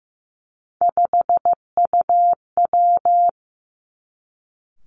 Morse code, heard as a keyed tone, about 15 wpm, 700 Hz. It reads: 5UW
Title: Three Challenging Letter-Number Combinations 15wpm